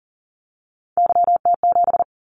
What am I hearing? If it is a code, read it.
Morse code, heard as a keyed tone, about 40 wpm, 700 Hz. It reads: YT7
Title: Callsign Prefixes for Top 25 DXCC - 40wpm